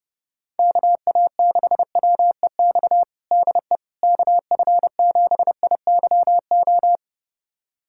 Morse code, heard as a keyed tone, about 30 wpm, 700 Hz. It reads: KA6WEX DE KF7IYO
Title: USA Callsign DE Callsign - 6 Characters - 30wpm